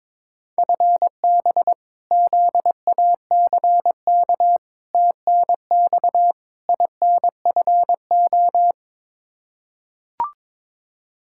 Morse code, heard as a keyed tone, about 22 wpm, 700 Hz. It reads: FB ZACK TNX INFO
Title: Practice QSO - Exchange 3 - More Info - 22wpm